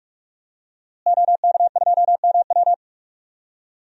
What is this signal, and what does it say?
Morse code, heard as a keyed tone, about 44 wpm, 700 Hz. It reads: OK2MW
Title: DXCC-100 Call Signs - 5 Characters - Repeated 3x - 45wpm